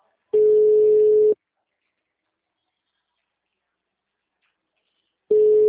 Ringback tone (Europe, 425 Hz).